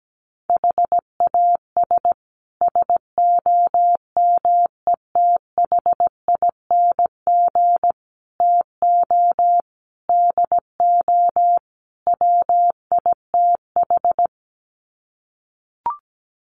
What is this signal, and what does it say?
Morse code, heard as a keyed tone, about 17 wpm, 700 Hz. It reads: HAS SOMETHING TO DO WITH
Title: Sets of 5 Words Encore 17wpm